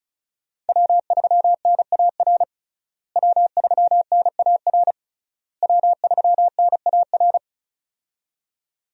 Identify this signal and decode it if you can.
Morse code, heard as a keyed tone, about 36 wpm, 700 Hz. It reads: W3NAR W3NAR W3NAR
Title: USA Call Signs - 5 Characters - Repeated 3x - 35wpm